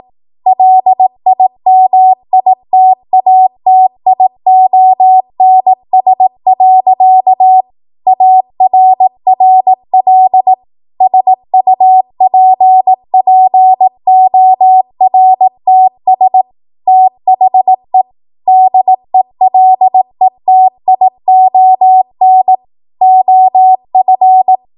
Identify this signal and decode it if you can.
Morse code, heard as a keyed tone, about 18 wpm, 750 Hz. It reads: LIMITATIONS. ARRL SUPPORTS THE DELETION OF
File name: Morse / 18WPM